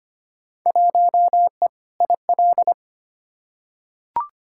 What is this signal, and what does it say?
Morse code, heard as a keyed tone, about 25 wpm, 700 Hz. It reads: 1E IL
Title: Field Day Exchanges - 25wpm